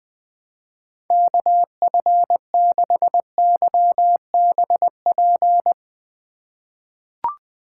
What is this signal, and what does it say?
Morse code, heard as a keyed tone, about 20 wpm, 700 Hz. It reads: KF6YBP
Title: USA Call Signs - 6 Characters - 3x Character Spacing - 20wpm